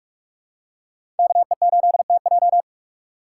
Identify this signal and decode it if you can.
Morse code, heard as a keyed tone, about 46 wpm, 700 Hz. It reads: KE8TJ
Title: USA Call Signs - 5 Characters - 6x Character Spacing - 45wpm